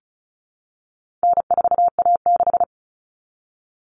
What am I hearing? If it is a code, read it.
Morse code, heard as a keyed tone, about 34 wpm, 700 Hz. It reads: N4A6
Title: Common Cut Numbers - 4-Digit 35wpm